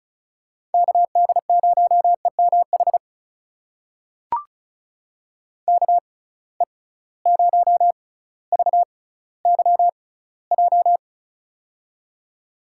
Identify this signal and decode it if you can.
Morse code, heard as a keyed tone, about 36 wpm, spaced as for 18 wpm, 700 Hz. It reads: KD0EMH KE0VYJ
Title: USA Call Signs - 6 Characters - 6x Character Spacing - 35wpm